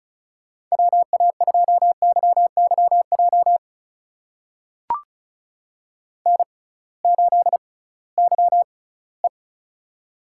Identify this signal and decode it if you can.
Morse code, heard as a keyed tone, about 34 wpm, spaced as for 20 wpm, 700 Hz. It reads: WA2YYJ N8YE
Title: USA Call Signs - 6x Character Spacing - 35wpm